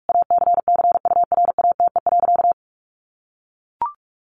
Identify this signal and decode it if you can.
Morse code, heard as a keyed tone, about 44 wpm, 700 Hz. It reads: ACCURATE.
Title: Sentences from Top 1200 Words 45wpm